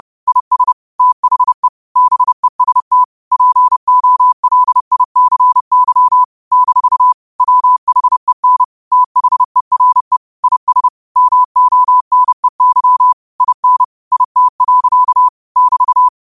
Morse code, heard as a keyed tone, about 30 wpm, 1000 Hz. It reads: IS THE BEST POLICY - WHEN THERE IS MONEY IN IT. <BT>